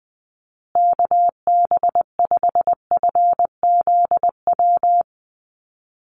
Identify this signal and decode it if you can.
Morse code, heard as a keyed tone, about 20 wpm, 700 Hz. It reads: KB5FZW
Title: USA Call Signs Repeated 3x 20wpm